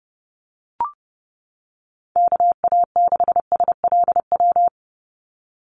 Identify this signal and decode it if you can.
Morse code, heard as a keyed tone, about 30 wpm, 700 Hz. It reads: KA6SLW
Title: USA Call Signs 30wpm